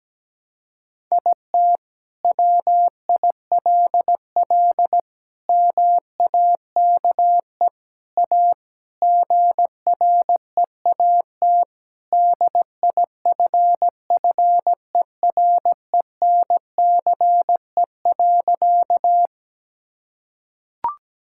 Morse code, heard as a keyed tone, about 17 wpm, 700 Hz. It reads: IT WILL MAKE A GREAT DIFFERENCE.
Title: Sentences from Top 1000 Words - Review Part 5 - 17wpm